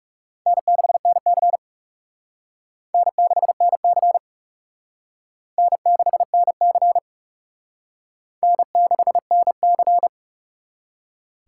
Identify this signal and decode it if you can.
Morse code, heard as a keyed tone, about 36 wpm, 700 Hz. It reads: N6NC N6NC N6NC N6NC
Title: Speed Racing - USA 1x2 Call Signs 30wpm